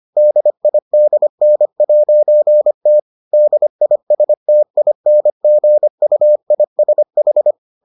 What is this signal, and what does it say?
Morse code, heard as a keyed tone, about 25 wpm, 600 Hz. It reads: DIDN'T DISTINGUISH